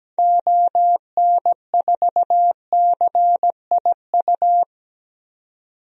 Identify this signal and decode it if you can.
Morse code, heard as a keyed tone, about 17 wpm, 700 Hz. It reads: ON4CIU
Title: DXCC-100 Call Signs - 6 Characters - 17wpm